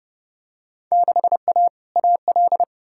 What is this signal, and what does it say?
Morse code, heard as a keyed tone, about 32 wpm, 700 Hz. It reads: 6A AL
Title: Field Day Exchanges Encore - 30wpm